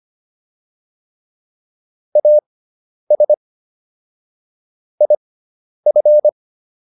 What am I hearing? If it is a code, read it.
Morse code, heard as a keyed tone, about 25 wpm, spaced as for 10 wpm, 600 Hz. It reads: AS IF